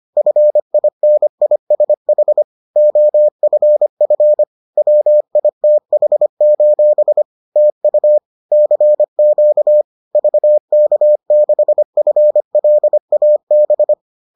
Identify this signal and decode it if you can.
Morse code, heard as a keyed tone, about 25 wpm, 600 Hz. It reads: FINISH OFF WITH: TU CQ VK6FLAB